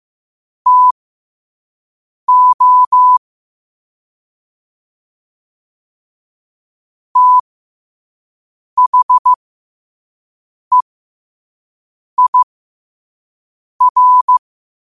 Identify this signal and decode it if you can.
Morse code, heard as a keyed tone, about 15 wpm, spaced as for 5 wpm, 1000 Hz. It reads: TO THEIR